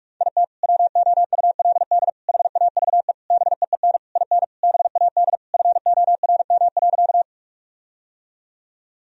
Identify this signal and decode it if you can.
Morse code, heard as a keyed tone, about 46 wpm, 700 Hz. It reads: IT WOULD HAVE BEEN IN BAD FORM.
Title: Sentences from Top 1000 Words - Review Part 1 - 45wpm